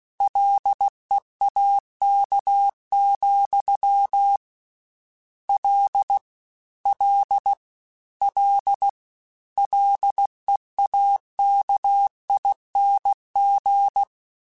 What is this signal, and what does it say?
Morse code, heard as a keyed tone, about 16 wpm, 800 Hz. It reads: LEAK, L L L LEAKING
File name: Combinations / Combinations-mp3s / Combinations@15